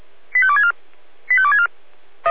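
Five-tone selective call (ZVEI), heard as quick runs of five tones, tones of about 70 ms. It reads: 85264 85264
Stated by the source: ZVEI five-tone selective call received by SDR.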